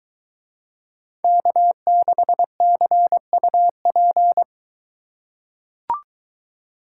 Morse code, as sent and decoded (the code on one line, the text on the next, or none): -.- -.... -.-. ..- .--.
K6CUP